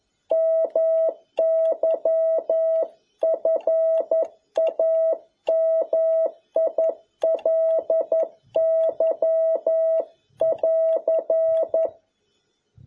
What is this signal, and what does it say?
Morse code, heard as a keyed tone, about 11 wpm, 650 Hz. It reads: MY FAMILY <AR>
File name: jh8cbh / 2021cw